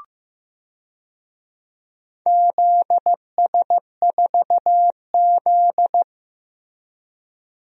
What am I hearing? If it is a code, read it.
Morse code, heard as a keyed tone, about 15 wpm, 700 Hz. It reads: ZS4Z
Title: DXCC-100 Call Signs - 4 Characters - Repeated 2x - 15wpm